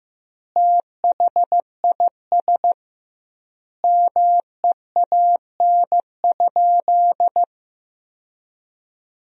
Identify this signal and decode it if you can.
Morse code, heard as a keyed tone, about 15 wpm, 700 Hz. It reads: THIS MEAN?
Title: 2x Word-Spacing - Sentences from Top 200 Words - 15wpm